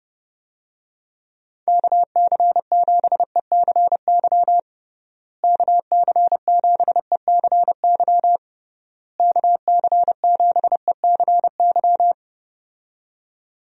Morse code, heard as a keyed tone, about 30 wpm, 700 Hz. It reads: KC7ECY KC7ECY KC7ECY
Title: USA Call Signs - 6 Characters - Repeated 3x - 30wpm